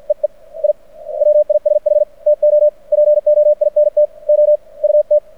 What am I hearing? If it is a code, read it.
Morse code, heard as a keyed tone, about 53 wpm, spaced as for 43 wpm, 600 Hz. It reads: EE I WISH TO POINT OUT
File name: archived / CRSnet